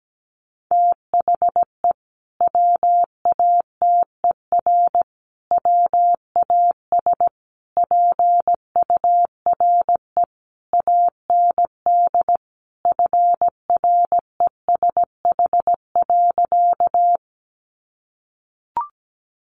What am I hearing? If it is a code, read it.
Morse code, heard as a keyed tone, about 17 wpm, 700 Hz. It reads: THE WATER WAS PURE AND FRESH.
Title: Sentences from Top 1500 Words 17wpm